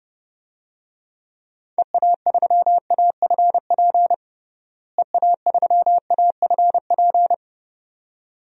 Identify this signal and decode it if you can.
Morse code, heard as a keyed tone, about 30 wpm, 700 Hz. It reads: EA3AFP EA3AFP
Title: DXCC-100 Call Signs - 6 Characters - Repeated 2x - 30wpm